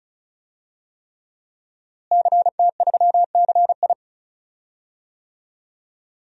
Morse code, heard as a keyed tone, about 35 wpm, 700 Hz. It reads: CT3CI
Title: DXCC-100 Call Signs - 5 Characters - Repeated 3x - 35wpm